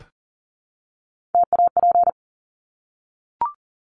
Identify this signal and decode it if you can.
Morse code, heard as a keyed tone, about 40 wpm, 700 Hz. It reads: TAP
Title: Top 1251-1500 Words 40wpm